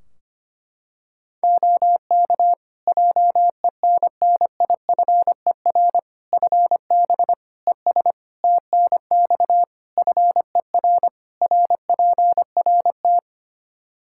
Morse code, heard as a keyed tone, about 25 wpm, 700 Hz. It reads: OK JENNIFER FB ES TNX FER RPRT
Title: Practice QSO - Exchange 2 - Info - 25wpm